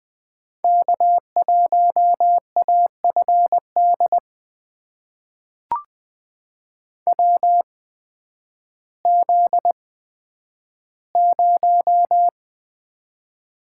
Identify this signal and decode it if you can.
Morse code, heard as a keyed tone, about 20 wpm, spaced as for 11 wpm, 700 Hz. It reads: K1AFD WZ0
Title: USA Call Signs - 8x Character Spacing - 20wpm